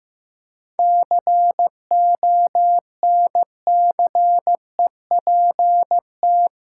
Morse code, heard as a keyed tone, about 15 wpm, 700 Hz. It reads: CONCEPT
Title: Seven Letter Words - 15wpm